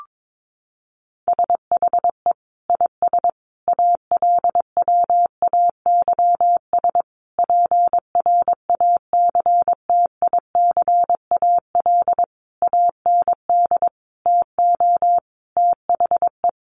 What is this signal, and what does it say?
Morse code, heard as a keyed tone, about 22 wpm, 700 Hz. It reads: SHE IS ALWAYS PRACTICAL AND TO THE
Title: Sentences from Top 1400 Words 22wpm